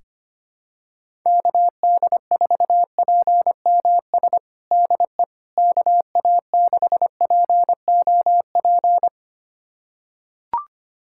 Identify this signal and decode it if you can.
Morse code, heard as a keyed tone, about 25 wpm, 700 Hz. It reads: KD4PMS DE KA6POP
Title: USA Callsign DE Callsign 25wpm